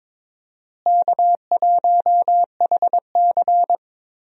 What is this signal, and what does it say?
Morse code, heard as a keyed tone, about 22 wpm, 700 Hz. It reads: K1HC
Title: USA Call Signs - 4 Characters - Repeated 2x - 22wpm